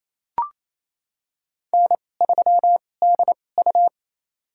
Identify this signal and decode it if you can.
Morse code, heard as a keyed tone, about 28 wpm, spaced as for 20 wpm, 700 Hz. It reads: N3DU
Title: USA Call Signs - 4 Characters - 2x Character Spacing - 28wpm